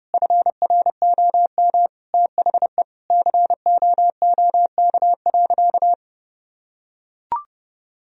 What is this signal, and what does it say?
Morse code, heard as a keyed tone, about 30 wpm, 700 Hz. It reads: FROM THE COOK.
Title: Sentences from Top 1000 Words - Review Part 4 - 30wpm